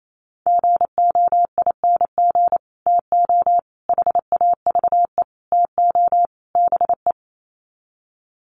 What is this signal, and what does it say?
Morse code, heard as a keyed tone, about 28 wpm, 700 Hz. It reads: GOING TO HAVE TO BE
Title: Sets of 5 Words 28wpm